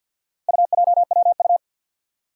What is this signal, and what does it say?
Morse code, heard as a keyed tone, about 46 wpm, 700 Hz. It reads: UJWU
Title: Four Challenging Letter-Number Combinations 50wpm